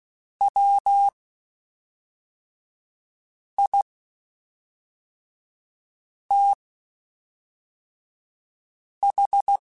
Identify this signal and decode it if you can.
Morse code, heard as a keyed tone, about 16 wpm, spaced as for 3 wpm, 800 Hz. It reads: WITH
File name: Spaced Characters / Spaced Characters-mp3s / Weather 3 17 sp @5